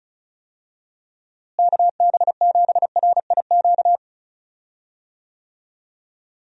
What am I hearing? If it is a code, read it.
Morse code, heard as a keyed tone, about 36 wpm, 700 Hz. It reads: KB7RIQ
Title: USA Call Signs - 6 Characters - Repeated 3x - 35wpm